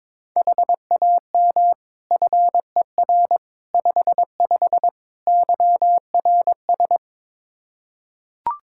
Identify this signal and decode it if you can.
Morse code, heard as a keyed tone, about 22 wpm, 700 Hz. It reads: HAM FER 55 YRS
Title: Practice QSO - Exchange 3 - More Info - 22wpm